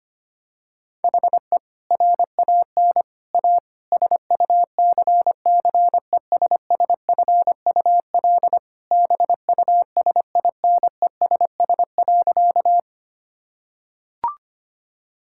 Morse code, heard as a keyed tone, about 25 wpm, 700 Hz. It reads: HE RAN A SUCCESSFUL BUSINESS.
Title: Sentences from Top 1000 Words - Review Part 6 - 25wpm